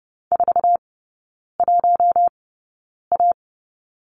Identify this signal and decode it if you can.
Morse code, heard as a keyed tone, about 30 wpm, spaced as for 9 wpm, 700 Hz. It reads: 41A
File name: Lesson 18e - Letter m - Three Characters 30wpm